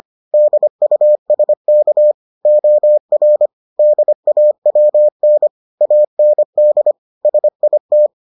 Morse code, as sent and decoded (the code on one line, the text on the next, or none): -.. ..- ... -.- / --- .-. / -.. .- .-- -. / .- -. -.. / ... .. -
DUSK OR DAWN AND SIT